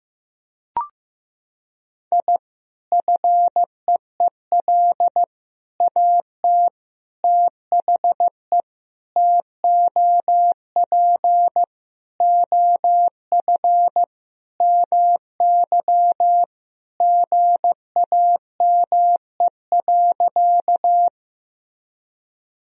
Morse code, heard as a keyed tone, about 15 wpm, 700 Hz. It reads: I FEEL AT THE TOP OF MY GAME.
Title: Sentences from Top 1000 Words - Review Part 2 - 15wpm